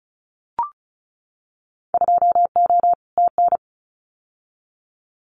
Morse code, heard as a keyed tone, about 34 wpm, 700 Hz. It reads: 2O TN
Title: Winter Field Day Exchanges 35wpm